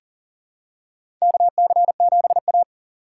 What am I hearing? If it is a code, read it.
Morse code, heard as a keyed tone, about 40 wpm, 700 Hz. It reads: KC7A
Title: USA Call Signs - 4 Characters - 6x Character Spacing - 40wpm